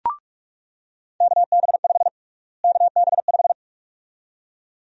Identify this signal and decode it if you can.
Morse code, heard as a keyed tone, about 46 wpm, 700 Hz. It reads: KB5 KB5
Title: USA Call Sign Prefixes - Repeated 2x - 45wpm